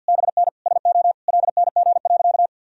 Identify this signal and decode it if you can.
Morse code, heard as a keyed tone, about 51 wpm, 700 Hz. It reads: BN SO LNG.
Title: Sentences from Top 100 Words and Abbreviations 50wpm